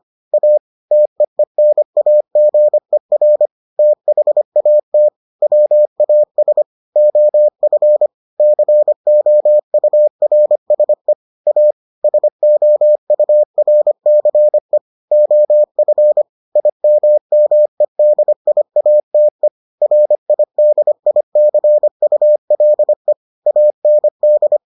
Morse code, heard as a keyed tone, about 25 wpm, 600 Hz. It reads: A TEENAGER THAT WAS OF COURSE A SOURCE OF IMMEDIATE RIDICULE AND